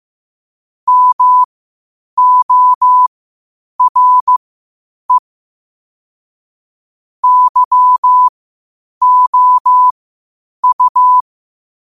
Morse code, heard as a keyed tone, about 15 wpm, spaced as for 8 wpm, 1000 Hz. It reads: MORE YOU
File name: qotdfiles / 8wpm